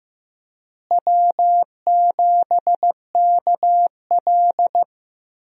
Morse code, heard as a keyed tone, about 15 wpm, 700 Hz. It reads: W7KL
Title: USA Call Signs - 4 Characters - Repeated 2x - 15wpm